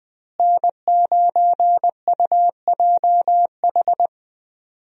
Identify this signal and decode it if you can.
Morse code, heard as a keyed tone, about 20 wpm, 700 Hz. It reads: N9UJH
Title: USA Call Signs - 5 Characters - Repeated 3x - 20wpm